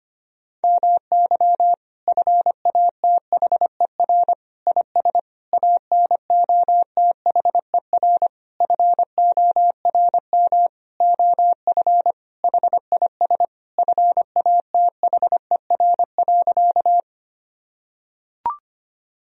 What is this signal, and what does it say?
Morse code, heard as a keyed tone, about 25 wpm, 700 Hz. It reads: MY FATHER IS ANOTHER FORM OF HIS FATHER.
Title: Sentences from Top 1000 Words - Review Part 1 - 25wpm